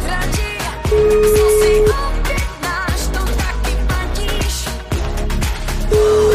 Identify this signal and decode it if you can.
Ringback tone (Europe, 425 Hz), with noise.